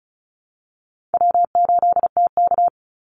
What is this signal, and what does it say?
Morse code, heard as a keyed tone, about 34 wpm, 700 Hz. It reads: W8TK
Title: Easy Call Signs Repeated 3x - 35wpm